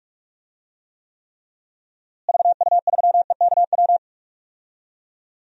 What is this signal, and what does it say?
Morse code, heard as a keyed tone, about 45 wpm, 700 Hz. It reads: VU3EKW
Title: DXCC-100 Call Signs - 6 Characters - Repeated 3x - 45wpm